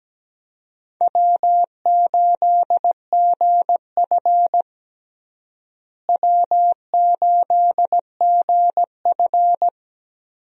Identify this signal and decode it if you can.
Morse code, heard as a keyed tone, about 17 wpm, 700 Hz. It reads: W8GF W8GF
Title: USA Call Signs - 4 Characters - Repeated 2x - 17wpm